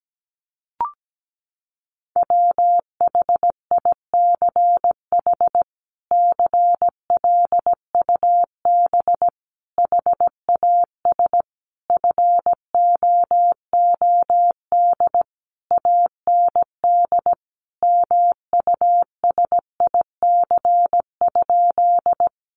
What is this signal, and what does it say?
Morse code, heard as a keyed tone, about 17 wpm, 700 Hz. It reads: WHICH CLUB HAS FOOD AND MUSIC?
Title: Sentences from Top 1200 Words 17wpm